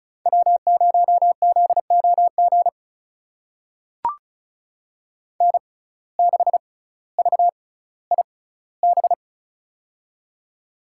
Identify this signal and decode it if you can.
Morse code, heard as a keyed tone, about 34 wpm, spaced as for 18 wpm, 700 Hz. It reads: W0ZOG N6VIB
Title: USA Call Signs - 5 Characters - 6x Character Spacing - 35wpm